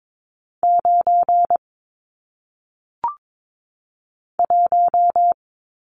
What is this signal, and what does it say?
Morse code, heard as a keyed tone, about 22 wpm, 700 Hz. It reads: 9 1
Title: Numbers - 1-Digit 22wpm